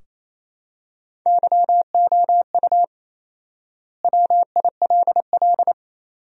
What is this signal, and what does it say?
Morse code, heard as a keyed tone, about 28 wpm, 700 Hz. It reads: YOU WILL
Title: 4x Word-Spacing - Sentences from Top 400 Words - 28wpm